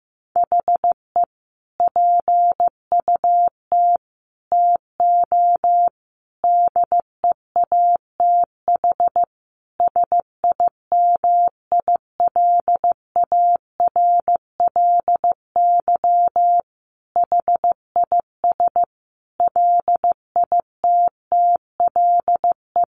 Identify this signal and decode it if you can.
Morse code, heard as a keyed tone, about 15 wpm, 700 Hz. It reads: HE PUT TO DEATH SIMILARLY HIS LITTLE
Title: Sentences from Top 1500 Words 15wpm